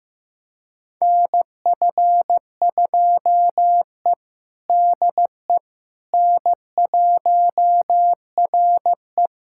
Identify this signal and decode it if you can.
Morse code, heard as a keyed tone, about 16 wpm, 700 Hz. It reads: NF2E DE N1RE
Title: USA Callsign DE Callsign - 4 Characters - 15wpm